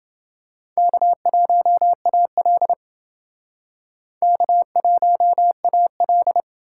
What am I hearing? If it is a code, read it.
Morse code, heard as a keyed tone, about 29 wpm, 700 Hz. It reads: K1AL K1AL